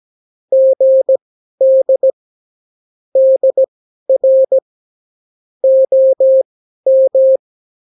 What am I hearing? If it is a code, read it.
Morse code, heard as a keyed tone, about 17 wpm, spaced as for 12 wpm, 530 Hz. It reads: GD DR OM
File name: CW QSO libroCW4 18-8ppm